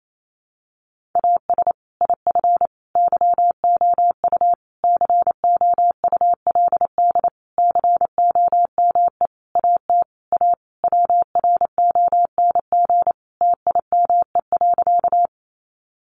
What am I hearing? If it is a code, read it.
Morse code, heard as a keyed tone, about 28 wpm, 700 Hz. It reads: AS IF YOU COULD COME AT A WRONG TIME.
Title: Sentences from Top 900 Words 28wpm